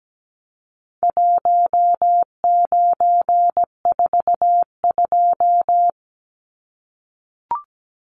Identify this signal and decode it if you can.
Morse code, heard as a keyed tone, about 17 wpm, 700 Hz. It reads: 1942
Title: Numbers - Contest Style - 4-Digit 17wpm